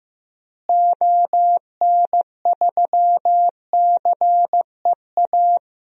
Morse code, heard as a keyed tone, about 15 wpm, 700 Hz. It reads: ON3CEA
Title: DXCC-100 Call Signs - 6 Characters - Repeated 3x - 15wpm